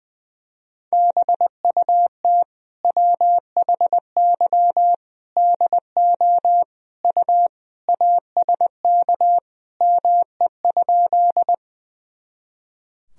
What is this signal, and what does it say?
Morse code, heard as a keyed tone, about 20 wpm, 700 Hz. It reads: BUT WHY DO U ASK ME?
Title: Sentences from Top 200 Words and Abbreviations 20wpm